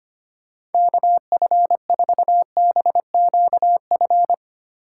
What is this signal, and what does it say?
Morse code, heard as a keyed tone, about 24 wpm, 700 Hz. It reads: KF4BQF
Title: Speed Racing - USA 2x3 Call Signs 25wpm